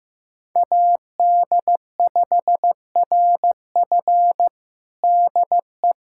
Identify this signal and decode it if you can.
Morse code, heard as a keyed tone, about 15 wpm, 700 Hz. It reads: AD5RF DE
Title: USA Callsign DE Callsign - 5 Characters - 15wpm